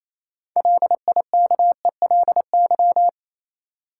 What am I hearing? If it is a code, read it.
Morse code, heard as a keyed tone, about 28 wpm, 700 Hz. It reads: LIKELY
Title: Sentences from Top 600 Words - New Word Drill - 28wpm